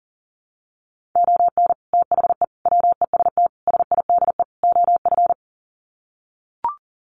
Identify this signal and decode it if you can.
Morse code, heard as a keyed tone, about 37 wpm, 700 Hz. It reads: ON THE WEST SIDE OF
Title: Sets of 5 Words Encore2 40wpm